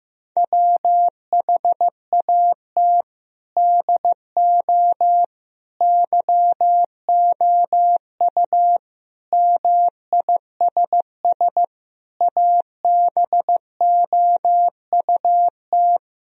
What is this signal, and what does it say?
Morse code, heard as a keyed tone, about 16 wpm, 700 Hz. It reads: WHAT DO YOU MISS ABOUT
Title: Sentences from Top 1000 Words - Review Part 3 - 15wpm